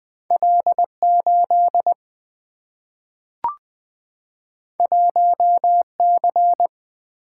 Morse code, heard as a keyed tone, about 20 wpm, 700 Hz. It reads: L8 1C
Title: Single-Letter Single-Number 20wpm